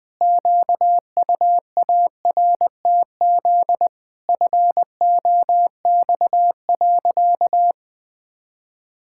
Morse code, heard as a keyed tone, about 20 wpm, 700 Hz. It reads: QUARTZ FOX.
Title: Pangram Challenge v2 - 20wpm